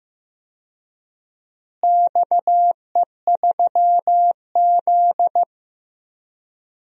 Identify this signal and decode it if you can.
Morse code, heard as a keyed tone, about 15 wpm, 700 Hz. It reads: XE3Z
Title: DXCC-100 Call Signs - 4 Characters - Repeated 2x - 15wpm